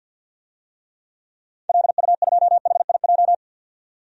Morse code, heard as a keyed tone, about 52 wpm, 700 Hz. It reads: LU2HIJ
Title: DXCC-100 Call Signs - 6 Characters - 50wpm